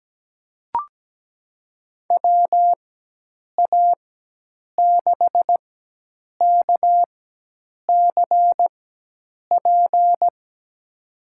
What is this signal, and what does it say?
Morse code, heard as a keyed tone, about 17 wpm, spaced as for 8 wpm, 700 Hz. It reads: WA6KCP
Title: USA Call Signs - 6 Characters - 4x Character Spacing - 17wpm